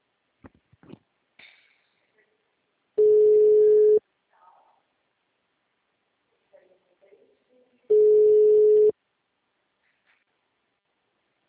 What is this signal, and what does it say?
Ringback tone (Europe, 425 Hz).